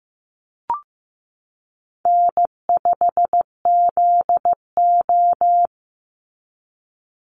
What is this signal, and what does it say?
Morse code, heard as a keyed tone, about 15 wpm, 700 Hz. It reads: N5ZO
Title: Call Signs Repeated 3x - 15wpm